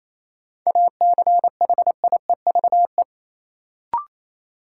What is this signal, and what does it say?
Morse code, heard as a keyed tone, about 28 wpm, 700 Hz. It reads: ACHIEVE
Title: Sentences from Top 1000 Words - Review Part 5 - New Word Drill - 28wpm